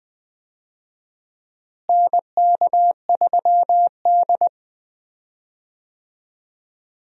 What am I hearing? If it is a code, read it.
Morse code, heard as a keyed tone, about 20 wpm, 700 Hz. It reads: NK3D
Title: USA Call Signs - 4 Characters - Repeated 2x - 20wpm